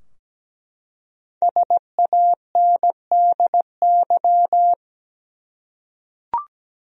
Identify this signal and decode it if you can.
Morse code, heard as a keyed tone, about 17 wpm, 700 Hz. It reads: SANDY
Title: USA Common Names - Max 5 Letters - 17wpm